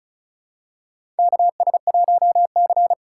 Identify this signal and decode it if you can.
Morse code, heard as a keyed tone, about 34 wpm, 700 Hz. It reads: KS1C
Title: USA Call Signs - 4 Characters - 4x Character Spacing - 35wpm